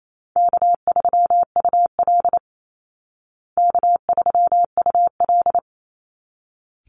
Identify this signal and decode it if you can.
Morse code, heard as a keyed tone, about 28 wpm, 700 Hz. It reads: K3UL K3UL
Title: Easy Call Signs Repeated 2x - 28wpm